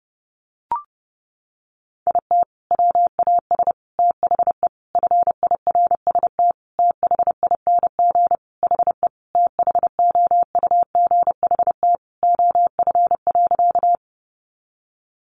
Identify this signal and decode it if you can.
Morse code, heard as a keyed tone, about 30 wpm, 700 Hz. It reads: IT WAS THE FIRST THING HE THOUGHT OF.
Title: Sentences from Top 300 Words 30wpm